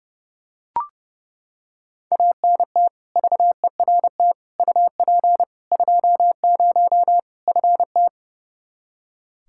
Morse code, heard as a keyed tone, about 30 wpm, 700 Hz. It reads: ANT VERT UP 20 FT
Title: K1USN SST QSO Practice 30wpm